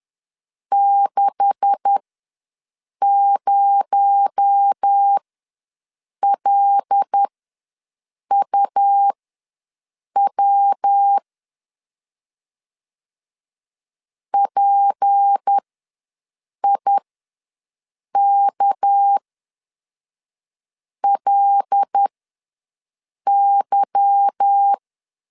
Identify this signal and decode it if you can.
Morse code, heard as a keyed tone, about 10 wpm, spaced as for 5 wpm, 800 Hz. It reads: 60LUW PIKLY
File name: LEZIONI / 9 286 ws 10wpm 4032